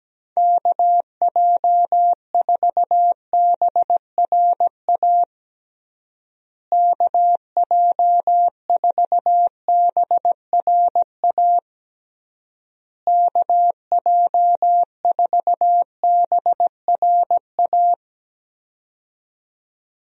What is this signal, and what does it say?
Morse code, heard as a keyed tone, about 17 wpm, 700 Hz. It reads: KJ4BRA KJ4BRA KJ4BRA
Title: USA Call Signs - 6 Characters - Repeated 3x - 17wpm